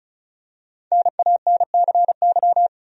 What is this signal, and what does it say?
Morse code, heard as a keyed tone, about 36 wpm, 700 Hz. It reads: NANCY
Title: USA Common Names - 35wpm